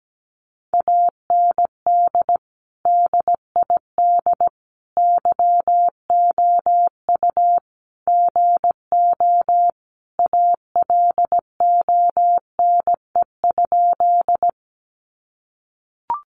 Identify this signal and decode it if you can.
Morse code, heard as a keyed tone, about 17 wpm, 700 Hz. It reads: AND DID YOU GO ALONE?
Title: Sentences from Top 1000 Words 17wpm